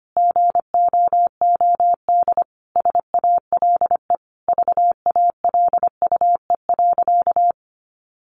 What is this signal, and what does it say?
Morse code, heard as a keyed tone, about 25 wpm, 700 Hz. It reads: GOOD SALE VALUE.
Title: Sentences from Top 1400 Words 25wpm